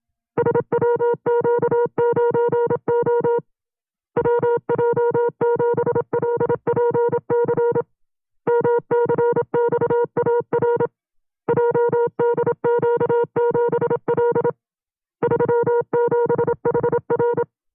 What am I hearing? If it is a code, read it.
Morse code, heard as a keyed tone, about 27 wpm, 480 Hz. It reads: SWQ9O WJ7LPC MCXAR JDQ7L 37HR
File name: FreeFiles / CW-practice / 25WPM-side2